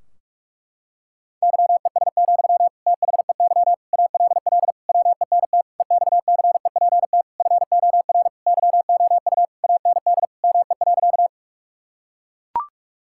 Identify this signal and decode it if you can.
Morse code, heard as a keyed tone, about 41 wpm, 700 Hz. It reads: YES, THEY ALL WENT EXCEPT FOR YOU AND ME.
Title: Sentences from Top 1600 Words 45wpm